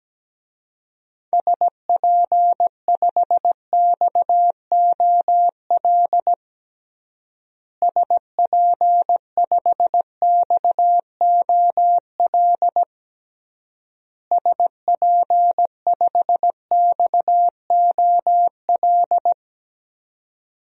Morse code, heard as a keyed tone, about 17 wpm, 700 Hz. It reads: SP5XOL SP5XOL SP5XOL
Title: DXCC-100 Call Signs - 6 Characters - Repeated 3x - 17wpm